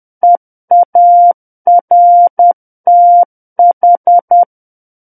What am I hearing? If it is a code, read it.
Morse code, heard as a keyed tone, about 10 wpm, 700 Hz. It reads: EARTH
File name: Book One / 01-Chapter 1 - 10 WPM